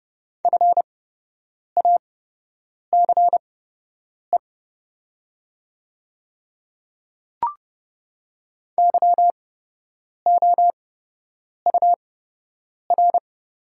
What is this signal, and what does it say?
Morse code, heard as a keyed tone, about 30 wpm, spaced as for 8 wpm, 700 Hz. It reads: FACE YOUR
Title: 8x Character Spacing - Top 300 Words - 30wpm